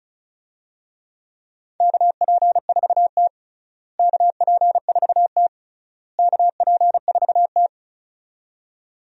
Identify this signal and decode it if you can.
Morse code, heard as a keyed tone, about 36 wpm, 700 Hz. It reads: KP4T KP4T KP4T
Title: DXCC-100 Call Signs - 4 Characters - Repeated 3x - 35wpm